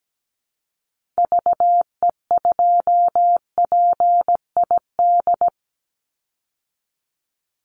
Morse code, heard as a keyed tone, about 17 wpm, 700 Hz. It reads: VE2PID
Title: Call Signs Repeated 3x - 17wpm